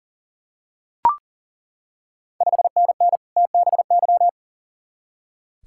Morse code, heard as a keyed tone, about 37 wpm, 690 Hz. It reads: 5NN TBY
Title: Ontario QSO Party Exchange 40wpm